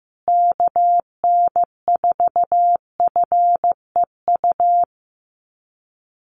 Morse code, heard as a keyed tone, about 15 wpm, 700 Hz. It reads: KN4FEU
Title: USA Call Signs Repeated 2x 15wpm